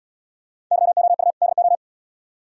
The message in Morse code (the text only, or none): YOU DO